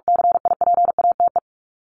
Morse code, heard as a keyed tone, about 44 wpm, 700 Hz. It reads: CIPATE
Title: Top 1501-1750 Words 45wpm